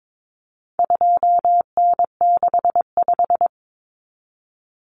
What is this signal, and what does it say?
Morse code, heard as a keyed tone, about 22 wpm, 700 Hz. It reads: 2N65
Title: Common Cut Numbers - 4-Digit 22wpm